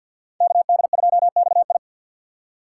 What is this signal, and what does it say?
Morse code, heard as a keyed tone, about 61 wpm, 700 Hz. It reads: KD2XI
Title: Speed Racing - USA Call Signs 50wpm